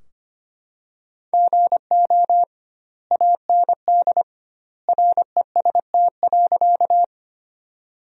Morse code, heard as a keyed tone, about 25 wpm, 700 Hz. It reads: GO AND REST.
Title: 2x Word-Spacing - Sentences from Top 500 Words - 25wpm